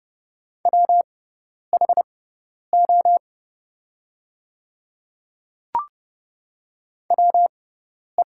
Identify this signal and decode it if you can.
Morse code, heard as a keyed tone, about 30 wpm, spaced as for 10 wpm, 700 Hz. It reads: WHO WE
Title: 6x Character Spacing - Top 100 Words - 30wpm